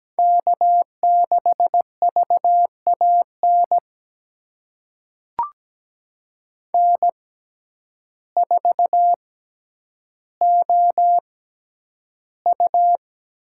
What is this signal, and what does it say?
Morse code, heard as a keyed tone, about 17 wpm, spaced as for 9 wpm, 700 Hz. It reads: K6VAN N4OU
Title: USA Call Signs - 5 Characters - 6x Character Spacing - 17wpm